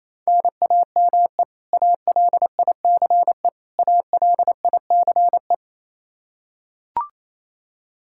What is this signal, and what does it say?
Morse code, heard as a keyed tone, about 28 wpm, 700 Hz. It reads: NAME ALICE ALICE
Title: Practice QSO - Exchange 1 - Beginning - 28wpm